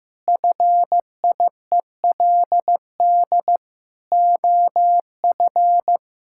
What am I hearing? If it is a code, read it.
Morse code, heard as a keyed tone, about 15 wpm, 700 Hz. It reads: FIELD OF
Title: Pangram Challenge v2 - 15wpm